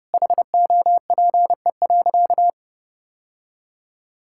Morse code, heard as a keyed tone, about 31 wpm, 700 Hz. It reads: HOPE.
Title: Sentences from Top 1000 Words - Review Part 5 - 30wpm